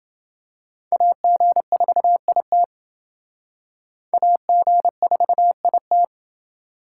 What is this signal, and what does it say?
Morse code, heard as a keyed tone, about 29 wpm, 700 Hz. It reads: AG4IT AG4IT